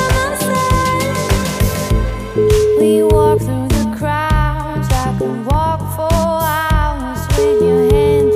Ringback tone (Europe, 425 Hz), with noise.